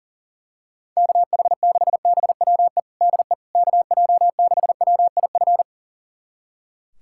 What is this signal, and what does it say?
Morse code, heard as a keyed tone, about 40 wpm, 700 Hz. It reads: KH6BWE DE KJ6WIF
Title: USA Callsign DE Callsign 40wpm